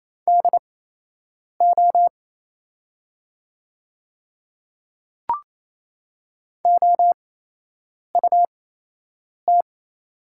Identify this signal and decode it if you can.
Morse code, heard as a keyed tone, about 27 wpm, spaced as for 8 wpm, 700 Hz. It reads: DO OUT
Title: 8x Character Spacing - Top 100 Words - 28wpm